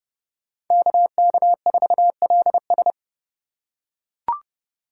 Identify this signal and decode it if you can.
Morse code, heard as a keyed tone, about 30 wpm, 700 Hz. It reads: KK4LS
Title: USA Call Signs - 5 Characters - 2x Character Spacing - 30wpm